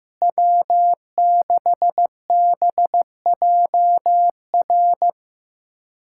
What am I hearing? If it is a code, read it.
Morse code, heard as a keyed tone, about 15 wpm, 700 Hz. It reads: W6BJR
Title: USA Callsign DE Callsign - 5 Characters - 15wpm